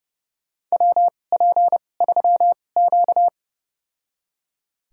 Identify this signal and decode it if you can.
Morse code, heard as a keyed tone, about 31 wpm, spaced as for 22 wpm, 700 Hz. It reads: WP3Q
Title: USA Call Signs - 4 Characters - 2x Character Spacing - 30wpm